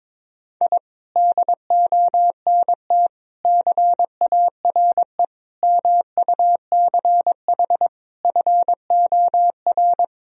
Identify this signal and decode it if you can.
Morse code, heard as a keyed tone, about 22 wpm, 700 Hz. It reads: I DONT CARE MUCH FOR
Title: Sentences from Top 1000 Words - Review Part 4 - 22wpm